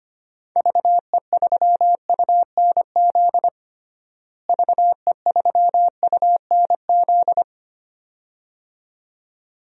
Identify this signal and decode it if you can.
Morse code, heard as a keyed tone, about 25 wpm, 700 Hz. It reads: VE3UNZ VE3UNZ
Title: DXCC-100 Call Signs - 6 Characters - Repeated 2x - 25wpm